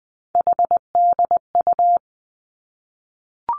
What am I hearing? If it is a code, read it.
Morse code, heard as a keyed tone, about 20 wpm, 700 Hz. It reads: HDU
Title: Three Challenging Letter-Number Combinations 20wpm